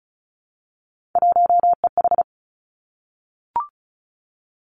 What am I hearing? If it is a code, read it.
Morse code, heard as a keyed tone, about 36 wpm, 700 Hz. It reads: 1EH
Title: Three Challenging Letter-Number Combinations 35wpm